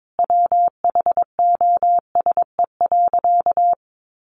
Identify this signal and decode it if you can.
Morse code, heard as a keyed tone, about 22 wpm, 700 Hz. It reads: WHOSE.
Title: Sentences from Top 1200 Words 22wpm